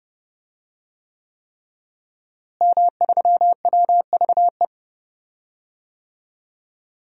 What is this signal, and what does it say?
Morse code, heard as a keyed tone, about 30 wpm, 700 Hz. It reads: M3WVE
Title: DXCC-100 Call Signs - 5 Characters - Repeated 3x - 30wpm